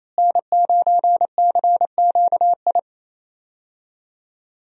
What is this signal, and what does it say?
Morse code, heard as a keyed tone, about 28 wpm, 700 Hz. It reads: N9CQI
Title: USA Call Signs - 5 Characters - Repeated 2x - 28wpm